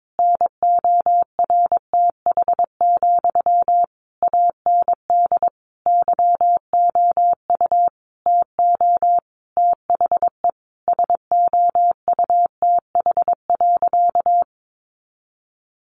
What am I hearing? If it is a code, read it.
Morse code, heard as a keyed tone, about 22 wpm, 700 Hz. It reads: NORTH, AND YOU TO THE SOUTH.
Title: Sentences from Top 400 Words 22wpm